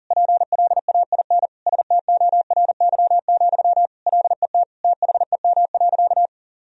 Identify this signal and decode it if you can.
Morse code, heard as a keyed tone, about 40 wpm, 700 Hz. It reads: PLAIN STORY, LET THEM.
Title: Sentences from Top 1000 Words - Review Part 2 - 40wpm